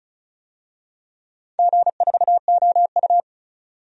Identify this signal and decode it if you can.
Morse code, heard as a keyed tone, about 36 wpm, 700 Hz. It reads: G4OU
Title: DXCC-100 Call Signs - 4 Characters - 35wpm